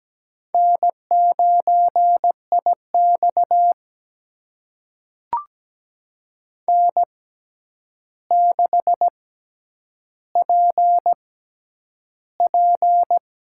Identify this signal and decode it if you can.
Morse code, heard as a keyed tone, about 17 wpm, spaced as for 9 wpm, 700 Hz. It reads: N9IX N6PP
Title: USA Call Signs - 4 Characters - 6x Character Spacing - 17wpm